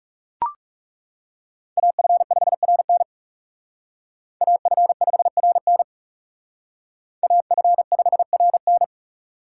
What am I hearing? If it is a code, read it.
Morse code, heard as a keyed tone, about 38 wpm, 700 Hz. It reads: AF5RN AF5RN AF5RN